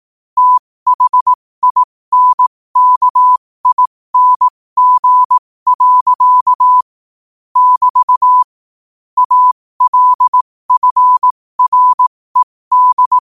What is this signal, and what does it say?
Morse code, heard as a keyed tone, about 18 wpm, spaced as for 15 wpm, 1000 Hz. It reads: THINKING. <BT> ALFRED
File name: qotdfiles / 15wpm